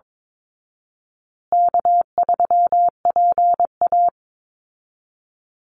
Morse code, heard as a keyed tone, about 22 wpm, 700 Hz. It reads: K3PA
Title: Easy Call Signs Repeated 3x - 22wpm